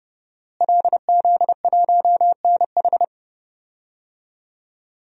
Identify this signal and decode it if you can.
Morse code, heard as a keyed tone, about 30 wpm, 700 Hz. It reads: LZ1NH
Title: DXCC-100 Call Signs - 5 Characters - Repeated 2x - 30wpm